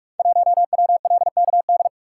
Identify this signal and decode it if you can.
Morse code, heard as a keyed tone, about 44 wpm, 700 Hz. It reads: 1WLKD
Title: Code Groups - 5 Letter:Numbers - 45wpm